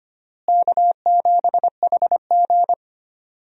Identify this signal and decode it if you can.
Morse code, heard as a keyed tone, about 25 wpm, 700 Hz. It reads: K7HG
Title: USA Call Signs - 4 Characters - Repeated 2x - 25wpm